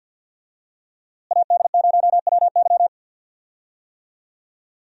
Morse code, heard as a keyed tone, about 47 wpm, 700 Hz. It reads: AD0WY
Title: USA Call Signs - 5 Characters - Repeated 3x - 50wpm